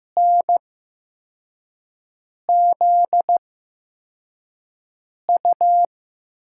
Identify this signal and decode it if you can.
Morse code, heard as a keyed tone, about 15 wpm, spaced as for 4 wpm, 700 Hz. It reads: NZU
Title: USA Call Signs - 6 Characters - 8x Character Spacing - 15wpm